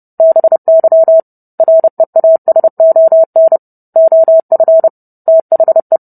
Morse code, heard as a keyed tone, about 30 wpm, 650 Hz. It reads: BY REASON OF THE